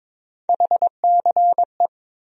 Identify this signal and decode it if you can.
Morse code, heard as a keyed tone, about 22 wpm, 700 Hz. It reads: HCE
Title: USA Call Sign Postfixes - 3 Letters 22wpm